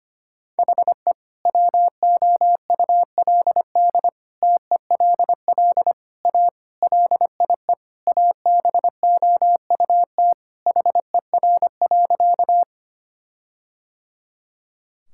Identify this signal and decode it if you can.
Morse code, heard as a keyed tone, about 25 wpm, 700 Hz. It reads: HE WOULD TELL A LIE ABOUT HER.
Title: Sentences from Top 1600 Words 25wpm